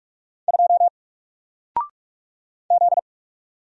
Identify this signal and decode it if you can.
Morse code, heard as a keyed tone, about 44 wpm, 700 Hz. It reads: 2 Z
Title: Single Letter-Number 45wpm